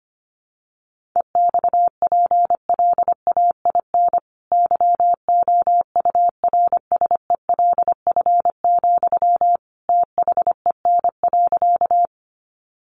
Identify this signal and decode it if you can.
Morse code, heard as a keyed tone, about 25 wpm, 700 Hz. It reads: EXPLAIN YOURSELF, THEN.
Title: Sentences from Top 1100 Words 25wpm